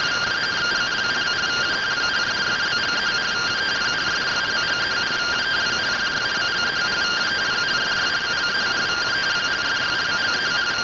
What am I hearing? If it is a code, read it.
Two-tone teleprinter signal, about 170 Hz shift.
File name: findings / MR AXIAL 71mm 3mm.gen-08 rtty rx